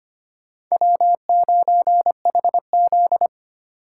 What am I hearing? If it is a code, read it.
Morse code, heard as a keyed tone, about 25 wpm, 700 Hz. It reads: W9HZ
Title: USA Call Signs - 4 Characters - 6x Character Spacing - 25wpm